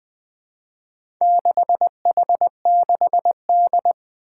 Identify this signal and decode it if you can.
Morse code, heard as a keyed tone, about 21 wpm, 700 Hz. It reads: 6H6D
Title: Four Challenging Letter-Number Combinations 20wpm